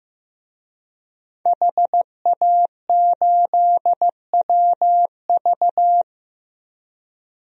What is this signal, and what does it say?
Morse code, heard as a keyed tone, about 15 wpm, 700 Hz. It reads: HA8WV
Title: DXCC-100 Call Signs - 5 Characters - Repeated 2x - 15wpm